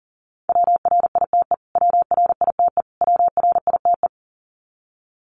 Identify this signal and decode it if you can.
Morse code, heard as a keyed tone, about 40 wpm, 700 Hz. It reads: WRITE WRITE WRITE
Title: Top 300 Words Repeated 40wpm